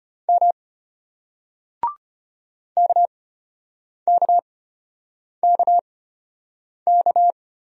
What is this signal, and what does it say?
Morse code, heard as a keyed tone, about 32 wpm, spaced as for 8 wpm, 700 Hz. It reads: M KKKK